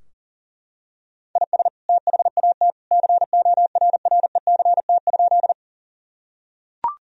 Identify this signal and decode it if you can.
Morse code, heard as a keyed tone, about 37 wpm, 700 Hz. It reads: IS THAT CORRECT?
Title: Sentences from Top 1000 Words - Review Part 2 - 40wpm